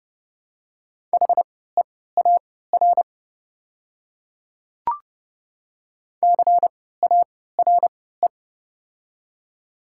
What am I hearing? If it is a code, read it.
Morse code, heard as a keyed tone, about 30 wpm, spaced as for 17 wpm, 700 Hz. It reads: HEAR CARE
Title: 3x Character Spacing - Top 300 Words - 30wpm